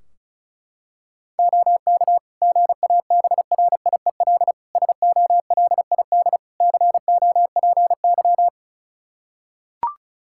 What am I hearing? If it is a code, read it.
Morse code, heard as a keyed tone, about 34 wpm, 700 Hz. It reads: OK GABRIEL SOLID COPY
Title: Practice QSO - Exchange 3 - More Info - 35wpm